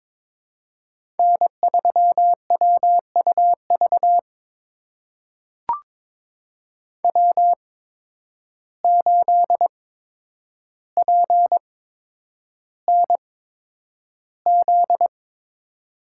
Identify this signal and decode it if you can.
Morse code, heard as a keyed tone, about 22 wpm, spaced as for 9 wpm, 700 Hz. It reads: N3WUV W8PNZ
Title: USA Call Signs - 5 Characters - 8x Character Spacing - 22wpm